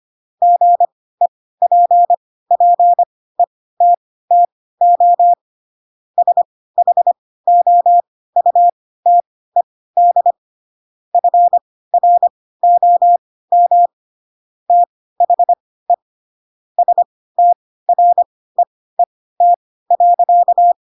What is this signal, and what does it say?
Morse code, heard as a keyed tone, about 25 wpm, spaced as for 16 wpm, 700 Hz. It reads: GEPPETTO SHOUTED FROM THE STREET.
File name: Pinocchio / Pinocchio ch7 10 25 ik1hge